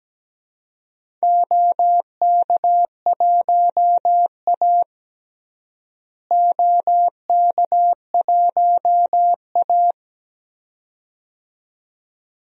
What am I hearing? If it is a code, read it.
Morse code, heard as a keyed tone, about 17 wpm, 700 Hz. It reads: OK1A OK1A
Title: DXCC-100 Call Signs - 4 Characters - Repeated 2x - 17wpm